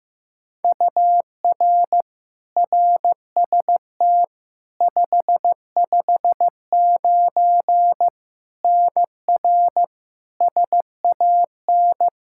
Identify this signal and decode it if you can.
Morse code, heard as a keyed tone, about 15 wpm, 700 Hz. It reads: UR RST 559 NR SAN
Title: K1USN SST QSO Practice 15wpm